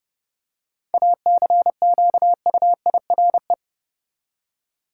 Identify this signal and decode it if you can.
Morse code, heard as a keyed tone, about 31 wpm, 700 Hz. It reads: ACQUIRE
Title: Seven Letter Words - 30wpm